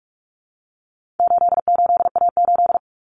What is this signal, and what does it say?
Morse code, heard as a keyed tone, about 44 wpm, 700 Hz. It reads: 88A8
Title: Common Cut Numbers - 4-Digit 45wpm